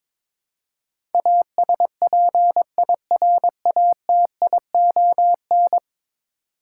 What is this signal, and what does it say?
Morse code, heard as a keyed tone, about 22 wpm, 700 Hz. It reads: ASPIRATION
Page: morse-code-ninja-practice-22wpm